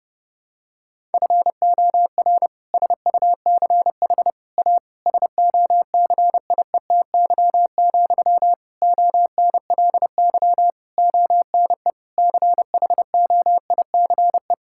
Morse code, heard as a keyed tone, about 32 wpm, 700 Hz. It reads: FOR SUCH A SOCIETY, ONLY ONE CHOICE
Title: Sentences from Top 1000 Words - Review Part 6 - 30wpm